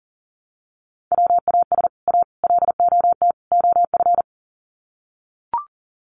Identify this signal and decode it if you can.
Morse code, heard as a keyed tone, about 40 wpm, 700 Hz. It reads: WAS A LOT OF
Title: Sets of 4 Words 40wpm